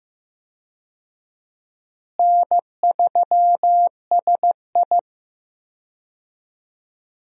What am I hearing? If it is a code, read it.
Morse code, heard as a keyed tone, about 15 wpm, 700 Hz. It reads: N3SI
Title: USA Call Signs - 4 Characters - Repeated 2x - 15wpm